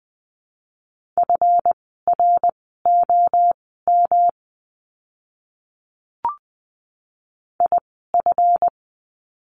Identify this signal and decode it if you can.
Morse code, heard as a keyed tone, about 20 wpm, spaced as for 14 wpm, 700 Hz. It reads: FROM IF
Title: 237 Lesson 38f - Letter q - Words 20wpm